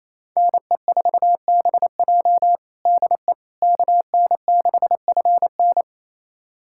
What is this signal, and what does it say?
Morse code, heard as a keyed tone, about 28 wpm, 700 Hz. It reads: NE4BJ DE KN6FN
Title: USA Callsign DE Callsign - 5 Characters - 28wpm